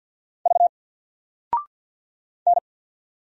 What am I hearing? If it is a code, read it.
Morse code, heard as a keyed tone, about 48 wpm, 700 Hz. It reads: V N
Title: Single Character (Letter:Number:Punctuation:Prosign) Alt 50wpm